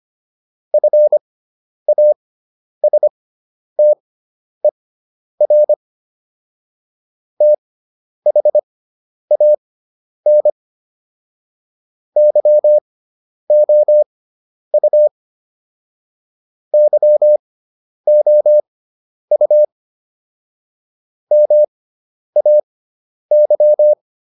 Morse code, heard as a keyed tone, about 25 wpm, spaced as for 10 wpm, 600 Hz. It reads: FASTER THAN YOU YOU MAY